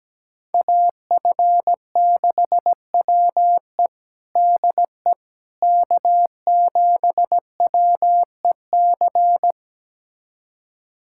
Morse code, heard as a keyed tone, about 17 wpm, 700 Hz. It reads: AF6WE DE K7WEC
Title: USA Callsign DE Callsign - 5 Characters - 17wpm